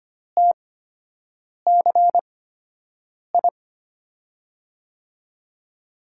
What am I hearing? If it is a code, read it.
Morse code, heard as a keyed tone, about 25 wpm, spaced as for 7 wpm, 700 Hz. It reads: TCI
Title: USA Call Signs - 8x Character Spacing - 25wpm